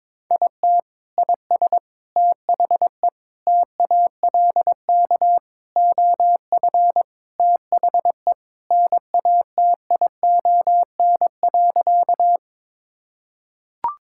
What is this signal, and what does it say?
Morse code, heard as a keyed tone, about 21 wpm, 700 Hz. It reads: IT IS THE TALK OF THE NATION.
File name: practice / 22wpm / Sentences with 4-Word Phrases - 22wpm